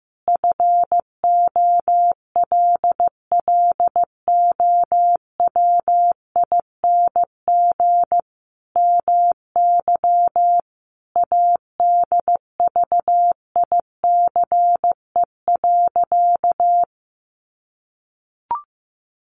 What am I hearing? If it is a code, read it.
Morse code, heard as a keyed tone, about 15 wpm, 700 Hz. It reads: FOLLOWING MY ADVICE.
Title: Sentences from Top 1100 Words 15wpm